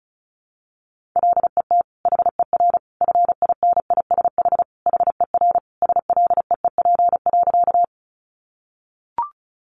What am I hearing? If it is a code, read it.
Morse code, heard as a keyed tone, about 34 wpm, 700 Hz. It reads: LET HER FINISH HER SLEEP.
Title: Sentences from Top 1200 Words 35wpm